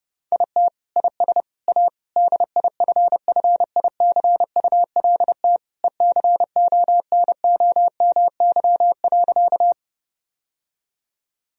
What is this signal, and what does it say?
Morse code, heard as a keyed tone, about 30 wpm, 700 Hz. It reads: IT IS A DIFFICULT ECONOMY.
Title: Sentences from Top 1000 Words - Review Part 3 - 30wpm